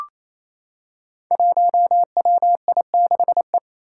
Morse code, heard as a keyed tone, about 28 wpm, 700 Hz. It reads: 1WI6E
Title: Five Challenging Letter-Number Combinations 28wpm